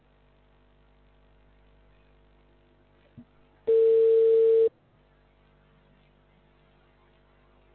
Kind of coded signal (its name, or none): telephone call progress tone